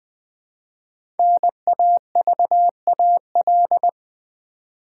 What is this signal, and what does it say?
Morse code, heard as a keyed tone, about 20 wpm, 700 Hz. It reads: NAVAL
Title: 5 Letter Words - First 3 Letters Given - 20wpm